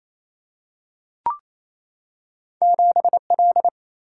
Morse code, heard as a keyed tone, about 28 wpm, 700 Hz. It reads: 7 L
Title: Single-Letter Single-Number 28wpm